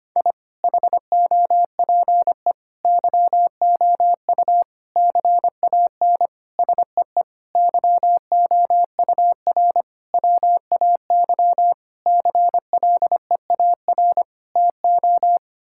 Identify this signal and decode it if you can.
Morse code, heard as a keyed tone, about 25 wpm, 700 Hz. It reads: I HOPE YOU CAN SEE YOUR WAY CLEAR TO
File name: Sets of 9 Words - 25wpm